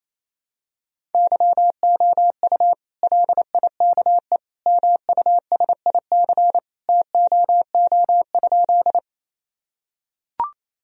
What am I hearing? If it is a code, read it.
Morse code, heard as a keyed tone, about 28 wpm, 700 Hz. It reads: YOU LIKE MUSIC TOO?
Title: Sentences from Top 1000 Words - Review Part 1 - 28wpm